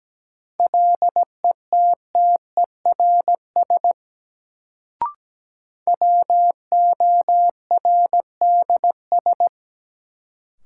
Morse code, heard as a keyed tone, about 17 wpm, 700 Hz. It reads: LETTERS WORDS
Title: Sentences from Top 1000 Words - Review Part 6 - New Word Drill - 17wpm